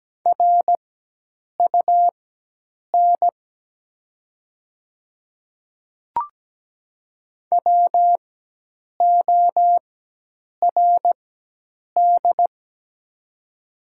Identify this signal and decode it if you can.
Morse code, heard as a keyed tone, about 17 wpm, spaced as for 8 wpm, 700 Hz. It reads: RUN WORD
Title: 4x Character Spacing - Top 500 Words - 17wpm